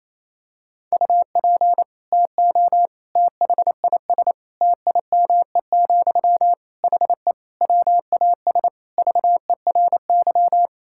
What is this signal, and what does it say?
Morse code, heard as a keyed tone, about 28 wpm, 700 Hz. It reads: UP TO THIS TIME, HE WAS VERY
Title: Sentences from Top 1000 Words - Review Part 6 - 28wpm